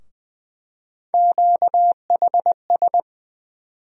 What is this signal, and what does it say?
Morse code, heard as a keyed tone, about 20 wpm, 700 Hz. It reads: QHS
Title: USA Call Sign Postfixes - 3 Letters 20wpm